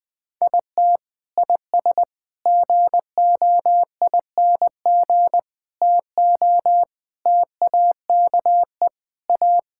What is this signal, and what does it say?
Morse code, heard as a keyed tone, about 20 wpm, 700 Hz. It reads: IT IS GOING TO TAKE A
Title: Sentences with 4-Word Phrases - 20wpm